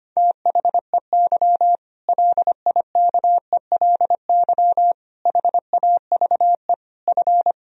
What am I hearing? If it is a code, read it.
Morse code, heard as a keyed tone, about 25 wpm, 700 Hz. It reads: THEY LIKELY HAVE F
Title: Sentences from Top 1000 Words - Review Part 4 - 25wpm